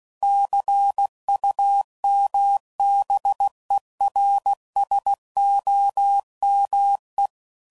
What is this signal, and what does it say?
Morse code, heard as a keyed tone, about 16 wpm, 800 Hz. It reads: CUMBERSOME
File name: CW Code Groups @15WPM / Technical Subjects / AND Gate